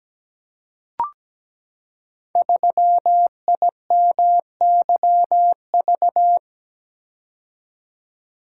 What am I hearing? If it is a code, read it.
Morse code, heard as a keyed tone, about 17 wpm, 700 Hz. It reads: 3IMYV
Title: Code Groups - 5 Letter:Numbers - 17wpm